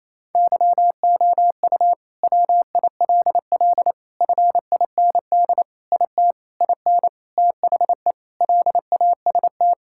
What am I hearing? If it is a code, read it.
Morse code, heard as a keyed tone, about 28 wpm, 700 Hz. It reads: YOU WILL FIND IT IN THE LAST
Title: Sentences from Top 1000 Words - Review Part 5 - 28wpm